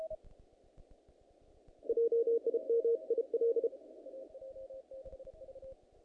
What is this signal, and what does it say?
Morse code, heard as a keyed tone, about 32 wpm, 460 Hz. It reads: JIM IL
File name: contests / naqp-cw-k5tr-op-k5pi- -2000